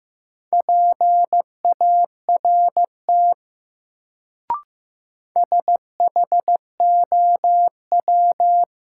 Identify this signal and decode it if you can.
Morse code, heard as a keyed tone, about 16 wpm, 700 Hz. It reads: PART SHOW
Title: Sentences from Top 200 Words - New Word Drill - 15wpm